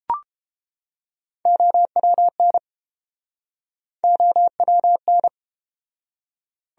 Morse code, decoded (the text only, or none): OWN OWN